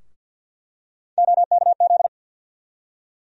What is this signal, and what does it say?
Morse code, heard as a keyed tone, about 49 wpm, 700 Hz. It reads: OKZ
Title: USA Call Sign Postfixes - 3 Letters 50wpm